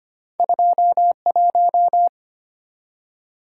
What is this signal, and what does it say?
Morse code, heard as a keyed tone, about 25 wpm, 700 Hz. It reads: 21
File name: Two Letter-Number Combinations - 25wpm